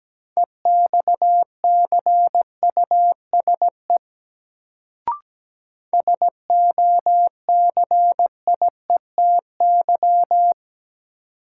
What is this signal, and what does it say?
Morse code, heard as a keyed tone, about 17 wpm, 700 Hz. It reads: EXCUSE SOCIETY
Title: Sentences from Top 700 Words - New Word Drill - 17wpm